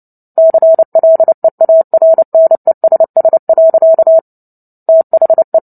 Morse code, heard as a keyed tone, about 30 wpm, 650 Hz. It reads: CLEARNESS. THE
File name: Book Two / 03-Chapter 3 - 29 WPM